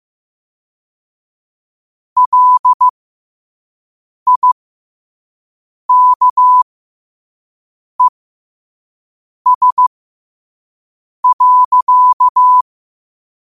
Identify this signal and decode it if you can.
Morse code, heard as a keyed tone, about 15 wpm, spaced as for 5 wpm, 1000 Hz. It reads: LIKES.
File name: qotdfiles / 5wpm